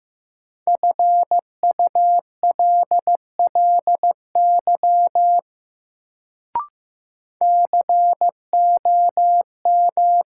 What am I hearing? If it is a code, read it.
Morse code, heard as a keyed tone, about 15 wpm, 700 Hz. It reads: FULLY COM
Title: Sentences from Top 1300 Words - New Word Drill - 15wpm